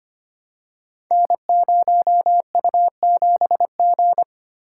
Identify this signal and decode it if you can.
Morse code, heard as a keyed tone, about 25 wpm, 700 Hz. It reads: N0U7G
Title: Code Groups - 5 Letter:Numbers - 25wpm